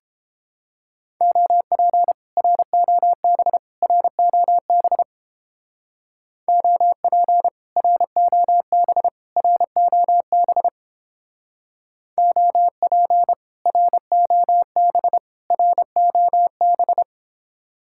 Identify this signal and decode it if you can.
Morse code, heard as a keyed tone, about 29 wpm, 700 Hz. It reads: OP ROB ROB OP ROB ROB OP ROB ROB